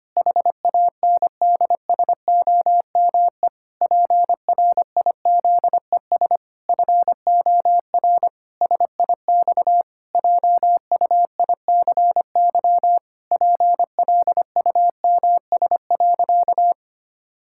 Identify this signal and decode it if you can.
Morse code, heard as a keyed tone, about 25 wpm, 700 Hz. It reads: HANDSOME PRIZES FOR SIX JUICY PLUMS.
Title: Pangram Challenge v2 - 25wpm